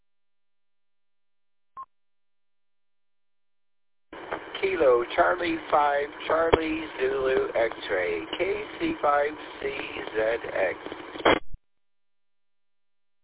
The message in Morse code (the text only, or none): none